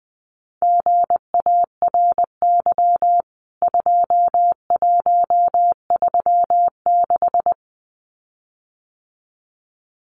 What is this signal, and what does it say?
Morse code, heard as a keyed tone, about 20 wpm, 700 Hz. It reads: GARY 2136
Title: CWT Exchange 20wpm